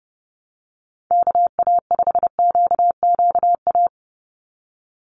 Morse code, heard as a keyed tone, about 30 wpm, 700 Hz. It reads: KA5QQA
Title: NA SOTA Chaser Callsigns 30wpm